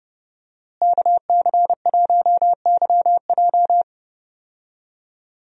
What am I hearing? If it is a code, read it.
Morse code, heard as a keyed tone, about 30 wpm, 700 Hz. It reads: KC1YJ
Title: USA Call Signs - 5 Characters - Repeated 3x - 30wpm